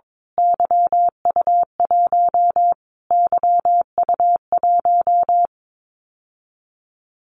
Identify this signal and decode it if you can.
Morse code, heard as a keyed tone, about 22 wpm, 700 Hz. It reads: YU1 YU1
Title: Callsign Prefixes for Top 25 DXCC - Repeated 3x - 22wpm